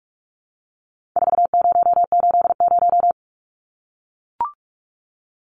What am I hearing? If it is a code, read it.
Morse code, heard as a keyed tone, about 44 wpm, 700 Hz. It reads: 4080
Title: Numbers - 4-Digit 45wpm